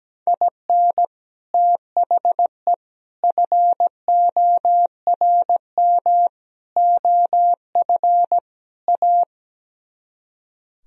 Morse code, heard as a keyed tone, about 17 wpm, 700 Hz. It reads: IN THE FORM OF A
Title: Speed-Racing - Sets of 5 Words 17wpm